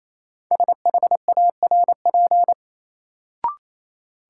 Morse code, heard as a keyed tone, about 28 wpm, 700 Hz. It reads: SHARP
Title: Sentences from Top 1600 Words - New Word Drill Alt - 28wpm